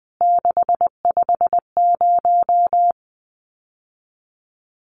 Morse code, heard as a keyed tone, about 20 wpm, 700 Hz. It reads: 650
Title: CWT Exchange 20wpm